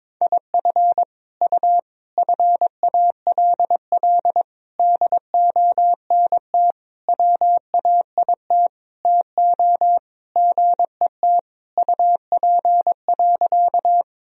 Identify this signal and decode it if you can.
Morse code, heard as a keyed tone, about 22 wpm, 700 Hz. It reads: IF U FALL DONT WAIT TO GET UP.
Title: Sentences from Top 500 Words and Abbreviations 22wpm